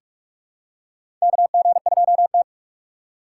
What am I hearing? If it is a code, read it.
Morse code, heard as a keyed tone, about 48 wpm, 700 Hz. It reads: KG2T
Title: USA Call Signs - 4 Characters - 8x Character Spacing - 45wpm